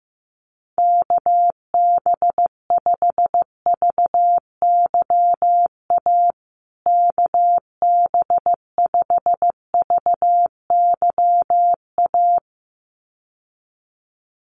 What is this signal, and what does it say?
Morse code, heard as a keyed tone, about 15 wpm, 700 Hz. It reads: KB5VYA KB5VYA
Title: USA Call Signs Repeated 2x 15wpm